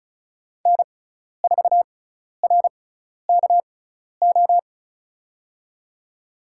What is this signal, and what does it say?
Morse code, heard as a keyed tone, about 36 wpm, spaced as for 12 wpm, 700 Hz. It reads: N4RKO
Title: USA Call Signs - 5 Characters - 6x Character Spacing - 35wpm